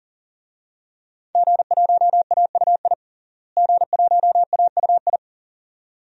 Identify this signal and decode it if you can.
Morse code, heard as a keyed tone, about 40 wpm, 700 Hz. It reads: G1AUI G1AUI
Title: DXCC-100 Call Signs - 5 Characters - Repeated 2x - 40wpm